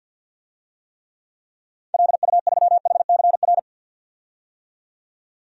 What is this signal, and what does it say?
Morse code, heard as a keyed tone, about 51 wpm, 700 Hz. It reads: LU3HCR
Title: DXCC-100 Call Signs - 6 Characters - 50wpm